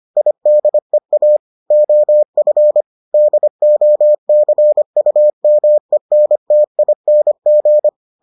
Morse code, heard as a keyed tone, about 25 wpm, 600 Hz. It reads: IDEA OF DOCUMENTING